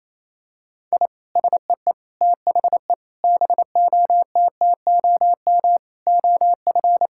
Morse code, heard as a keyed tone, about 28 wpm, 700 Hz. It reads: I SEE THE BOTTOM OF
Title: Sentences from Top 1000 Words - Review Part 6 - 28wpm